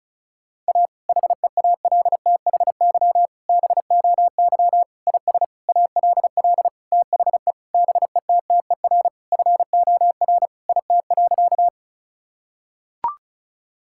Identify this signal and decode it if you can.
Morse code, heard as a keyed tone, about 36 wpm, 700 Hz. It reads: A HEALTHY BOY IS ALL THE BETTER FOR IT.
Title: Sentences from Top 1000 Words - Review Part 5 - 35wpm